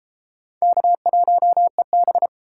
Morse code, heard as a keyed tone, about 31 wpm, 700 Hz. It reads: K1EB